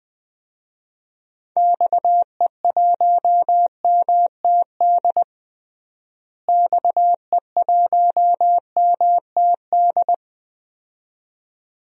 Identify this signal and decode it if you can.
Morse code, heard as a keyed tone, about 20 wpm, 700 Hz. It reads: XE1MTD XE1MTD
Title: DXCC-100 Call Signs - 6 Characters - Repeated 2x - 20wpm